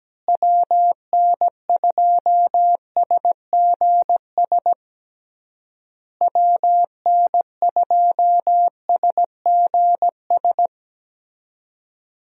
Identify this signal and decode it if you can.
Morse code, heard as a keyed tone, about 17 wpm, 700 Hz. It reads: WN2SGS WN2SGS
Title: USA Call Signs - 6 Characters - Repeated 3x - 17wpm